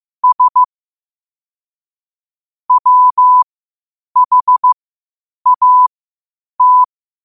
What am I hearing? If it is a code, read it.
Morse code, heard as a keyed tone, about 15 wpm, spaced as for 8 wpm, 1000 Hz. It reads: S WHAT
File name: qotdfiles / 8wpm